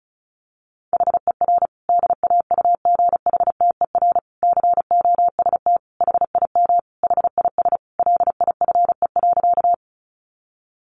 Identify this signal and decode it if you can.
Morse code, heard as a keyed tone, about 34 wpm, 700 Hz. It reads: HER DAUGHTER COST HIM HIS LIFE.
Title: Sentences from Top 1500 Words 35wpm